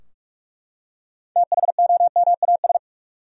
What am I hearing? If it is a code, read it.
Morse code, heard as a keyed tone, about 42 wpm, 700 Hz. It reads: THOMAS
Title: USA Common Names - 45wpm